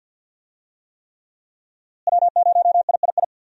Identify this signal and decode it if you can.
Morse code, heard as a keyed tone, about 52 wpm, 700 Hz. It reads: W0III
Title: USA Call Signs - 5 Characters - Repeated 3x - 50wpm